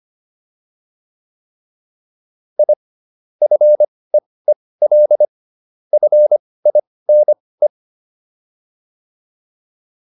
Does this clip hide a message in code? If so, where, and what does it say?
Yes: Morse code from 2.6 to 7.7 s, heard as a keyed tone, about 25 wpm, spaced as for 18 wpm, 600 Hz. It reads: I FEEL FINE